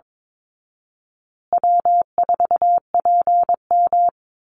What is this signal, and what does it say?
Morse code, heard as a keyed tone, about 22 wpm, 700 Hz. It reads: W4PM
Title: Easy Call Signs Repeated 2x - 22wpm